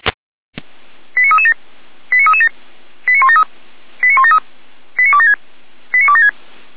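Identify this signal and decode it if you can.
Five-tone selective call (ZVEI), heard as quick runs of five tones, tones of about 80 ms. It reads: 89337 89337 89173 89173 89276 89276
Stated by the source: ZVEI five-tone call 89017 received by SDR on channel 455.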